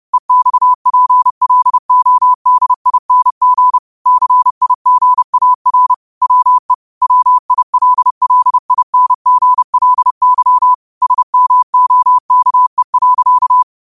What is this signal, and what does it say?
Morse code, heard as a keyed tone, about 30 wpm, 1000 Hz. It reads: EXPLODING CIGAR WE WILLINGLY SMOKE.
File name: qotdfiles / 30wpm